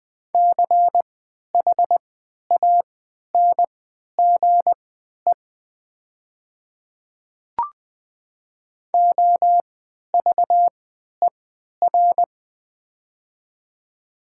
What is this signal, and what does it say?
Morse code, heard as a keyed tone, about 20 wpm, spaced as for 11 wpm, 700 Hz. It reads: CHANGE OVER
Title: 3x Character Spacing - Top 300 Words - 20wpm